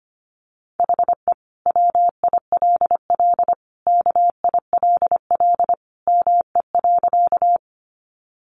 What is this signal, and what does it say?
Morse code, heard as a keyed tone, about 25 wpm, 700 Hz. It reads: HE WILL KILL ME.
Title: Sentences from Top 1400 Words 25wpm